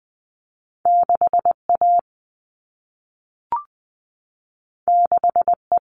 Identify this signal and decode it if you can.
Morse code, heard as a keyed tone, about 20 wpm, 700 Hz. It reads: 6 A 6 E
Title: All Cut Numbers - 2-Digit 20wpm